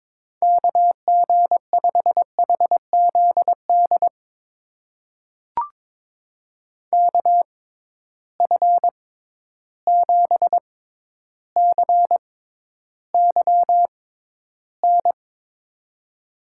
Morse code, heard as a keyed tone, about 22 wpm, spaced as for 11 wpm, 700 Hz. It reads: KG5HZD KF7CYN
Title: USA Call Signs - 6 Characters - 6x Character Spacing - 22wpm